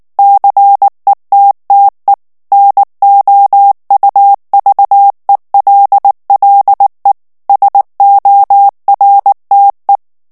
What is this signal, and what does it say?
Morse code, heard as a keyed tone, about 19 wpm, 800 Hz. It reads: CETTE NOUVELLE SORTE
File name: CW / FICHIERS-MP3 / FRANCAIS TEXTE CLAIR 2 VITESSE 18 MOTS MINUTES